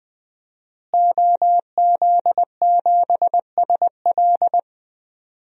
Morse code, heard as a keyed tone, about 20 wpm, 700 Hz. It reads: OZ7SL
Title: DXCC-100 Call Signs - 5 Characters - Repeated 2x - 20wpm